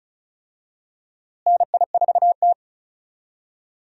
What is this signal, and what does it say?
Morse code, heard as a keyed tone, about 36 wpm, 700 Hz. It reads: NI4T
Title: USA Call Signs - 4 Characters - Repeated 2x - 35wpm